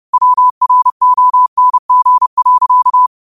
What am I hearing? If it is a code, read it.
Morse code, heard as a keyed tone, about 30 wpm, 1000 Hz. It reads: WRONG.